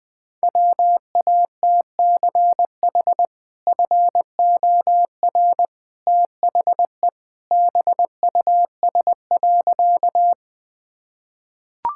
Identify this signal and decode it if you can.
Morse code, heard as a keyed tone, about 20 wpm, 700 Hz. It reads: WATCH FOR THE BUS.
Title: Sentences from Top 1000 Words - Review Part 6 - 20wpm